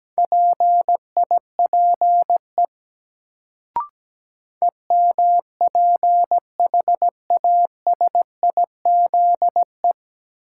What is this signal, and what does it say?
Morse code, heard as a keyed tone, about 17 wpm, 700 Hz. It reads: PIPE EMPHASIZE
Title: Sentences from Top 1500 Words - New Word Drill Alt - 17wpm